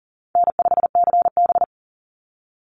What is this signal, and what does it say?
Morse code, heard as a keyed tone, about 40 wpm, 700 Hz. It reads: N5CB
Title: USA Call Signs 40wpm